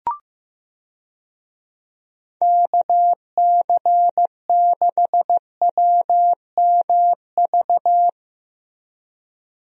Morse code, heard as a keyed tone, about 14 wpm, 700 Hz. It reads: KC6WMV
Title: USA Call Signs - 6 Characters - Repeated 3x - 15wpm